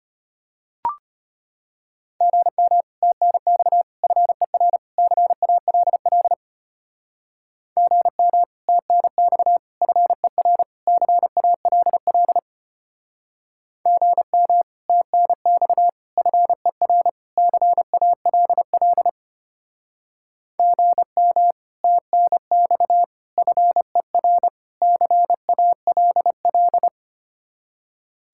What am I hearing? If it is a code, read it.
Morse code, heard as a keyed tone, about 33 wpm, 700 Hz. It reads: GM TNX FER CALL GM TNX FER CALL GM TNX FER CALL GM TNX FER CALL